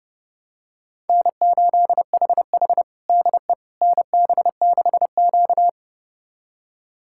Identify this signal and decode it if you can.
Morse code, heard as a keyed tone, about 31 wpm, 700 Hz. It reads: N8HH DE NB6Q
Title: USA Callsign DE Callsign - 4 Characters - 30wpm